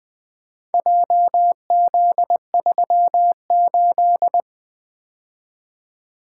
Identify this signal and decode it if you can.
Morse code, heard as a keyed tone, about 24 wpm, 700 Hz. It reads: JZ38
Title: Speed-Racing - Four Letter-Number Combinations - 20wpm